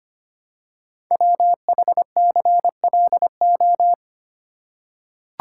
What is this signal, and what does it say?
Morse code, heard as a keyed tone, about 25 wpm, 700 Hz. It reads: WHCLO
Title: Code Groups - 5 Letter:Numbers - 25wpm